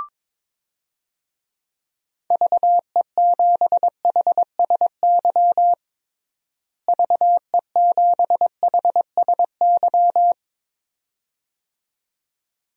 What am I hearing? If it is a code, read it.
Morse code, heard as a keyed tone, about 22 wpm, 700 Hz. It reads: VE7HSY VE7HSY
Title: DXCC-100 Call Signs - 6 Characters - Repeated 2x - 22wpm